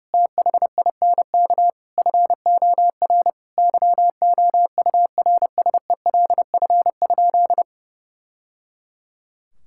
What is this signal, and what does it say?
Morse code, heard as a keyed tone, about 30 wpm, 700 Hz. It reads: THINK FOR YOURSELF?
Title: Sentences from Top 1000 Words - Review Part 6 - 30wpm